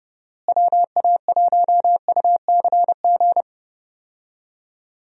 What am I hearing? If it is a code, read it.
Morse code, heard as a keyed tone, about 30 wpm, 700 Hz. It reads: WA1UCG
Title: USA Call Signs - 6 Characters - Repeated 2x - 30wpm